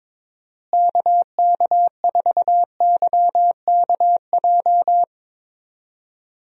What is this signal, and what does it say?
Morse code, heard as a keyed tone, about 22 wpm, 700 Hz. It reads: KK4YKJ
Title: USA Call Signs - 6 Characters - Repeated 2x - 22wpm